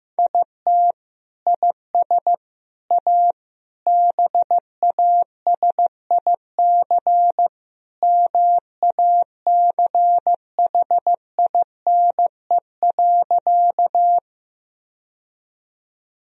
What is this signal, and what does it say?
Morse code, heard as a keyed tone, about 15 wpm, 700 Hz. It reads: IT IS A BASIC MACHINE.
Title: Sentences from Top 1000 Words - Review Part 3 - 15wpm